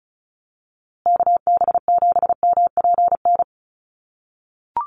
Morse code, heard as a keyed tone, about 34 wpm, 700 Hz. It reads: KB7MPN
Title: USA Call Signs Repeated 3x 35wpm